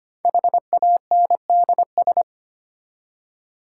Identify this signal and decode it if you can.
Morse code, heard as a keyed tone, about 25 wpm, 700 Hz. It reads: HANDS
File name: practice / 25wpm / Easy Words with Hints - 25wpm